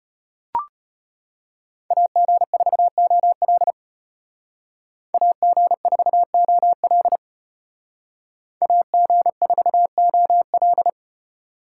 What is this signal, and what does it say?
Morse code, heard as a keyed tone, about 30 wpm, 700 Hz. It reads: AG4OL AG4OL AG4OL
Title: Speed Racing - USA 2x2 Call Signs 25wpm